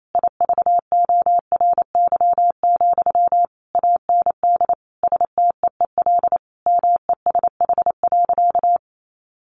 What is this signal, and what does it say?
Morse code, heard as a keyed tone, about 28 wpm, 700 Hz. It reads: IVORY, AND STEEL MESH.
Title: A Sound Of Thunder 28wpm